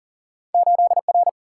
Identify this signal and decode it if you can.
Morse code, heard as a keyed tone, about 39 wpm, 700 Hz. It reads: 8R
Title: Single-Letter Single-Number 40wpm